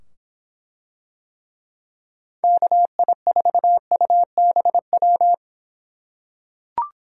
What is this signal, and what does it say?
Morse code, heard as a keyed tone, about 24 wpm, 700 Hz. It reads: KI4UBW